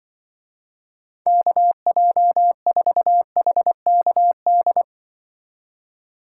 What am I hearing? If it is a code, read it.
Morse code, heard as a keyed tone, about 24 wpm, 700 Hz. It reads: KJ4HKD
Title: Speed Racing - USA 2x3 Call Signs 20wpm